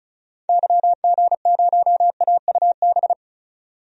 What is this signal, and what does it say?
Morse code, heard as a keyed tone, about 34 wpm, 700 Hz. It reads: YG0AUB
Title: DXCC-100 Call Signs - 6 Characters - 35wpm